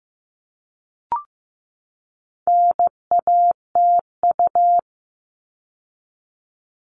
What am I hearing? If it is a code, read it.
Morse code, heard as a keyed tone, about 15 wpm, 700 Hz. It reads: NATU
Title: All Cut Numbers - 4-Digit 15wpm